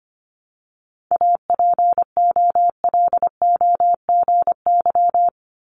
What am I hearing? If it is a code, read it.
Morse code, heard as a keyed tone, about 25 wpm, 700 Hz. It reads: APOLOGY
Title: Multi-letter Phonemes Repeated 3x 25wpm